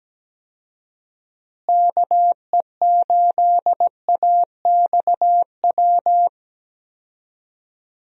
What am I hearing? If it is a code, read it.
Morse code, heard as a keyed tone, about 17 wpm, 700 Hz. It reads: KE8AXW
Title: USA Call Signs - 6 Characters - Repeated 3x - 17wpm